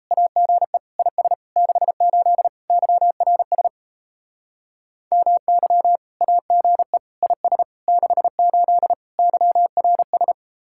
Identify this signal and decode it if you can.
Morse code, heard as a keyed tone, about 34 wpm, 700 Hz. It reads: AGE IS 68 YRS MY AGE IS 68 YRS
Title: Speed-Racing Common QSO Phrases Encore - 28wpm